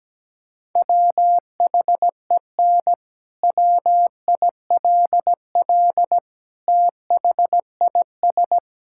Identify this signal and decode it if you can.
Morse code, heard as a keyed tone, about 17 wpm, 700 Hz. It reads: WHEN WILL THIS
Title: Sentences from Top 1600 Words 17wpm